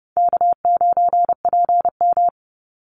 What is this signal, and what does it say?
Morse code, heard as a keyed tone, about 30 wpm, 700 Hz. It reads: K9PM
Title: NA SOTA Activator Callsigns 30wpm